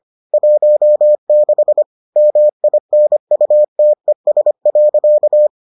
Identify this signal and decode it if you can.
Morse code, heard as a keyed tone, about 25 wpm, 600 Hz. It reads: 16 MINUTES.